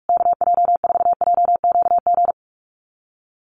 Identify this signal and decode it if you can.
Morse code, heard as a keyed tone, about 46 wpm, 700 Hz. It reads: KJ4JQG
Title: USA Call Signs Repeated 3x 45wpm